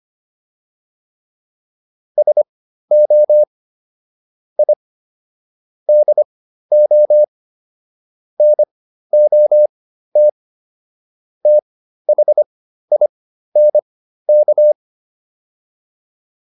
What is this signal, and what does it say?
Morse code, heard as a keyed tone, about 25 wpm, spaced as for 13 wpm, 600 Hz. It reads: SO I DO NOT THINK